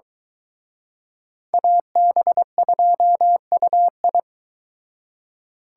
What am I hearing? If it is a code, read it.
Morse code, heard as a keyed tone, about 22 wpm, 700 Hz. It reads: AB2UI